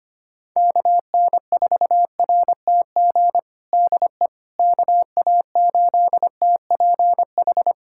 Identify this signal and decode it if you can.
Morse code, heard as a keyed tone, about 25 wpm, 700 Hz. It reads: KN4RTG DE KA8TPH
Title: USA Callsign DE Callsign - 6 Characters - 25wpm